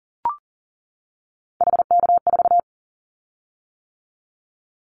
Morse code, heard as a keyed tone, about 40 wpm, 690 Hz. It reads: HK4
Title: Callsign Prefixes for Top 100 DXCC - 40wpm